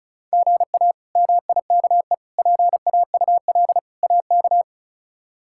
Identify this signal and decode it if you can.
Morse code, heard as a keyed tone, about 34 wpm, 700 Hz. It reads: GA MIKE PAUL AK
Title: K1USN SST QSO Practice - 35wpm